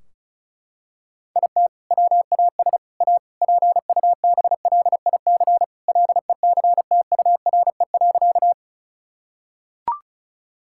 Morse code, heard as a keyed tone, about 36 wpm, 700 Hz. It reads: IT WAS A PUBLIC LECTURE.
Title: Sentences from Top 1600 Words 35wpm